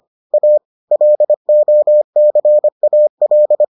Morse code, heard as a keyed tone, about 25 wpm, 600 Hz. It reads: A LOCAL